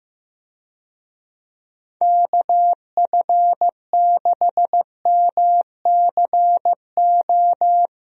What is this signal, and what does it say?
Morse code, heard as a keyed tone, about 15 wpm, 700 Hz. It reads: KF6MCO
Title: USA Call Signs - 6 Characters - Repeated 3x - 15wpm